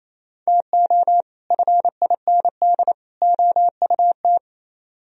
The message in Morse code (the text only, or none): TO FIND OUT